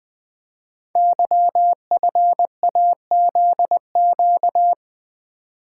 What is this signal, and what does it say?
Morse code, heard as a keyed tone, about 20 wpm, 700 Hz. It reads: YFAZQ
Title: Code Groups - 5 Letter:Numbers - 20wpm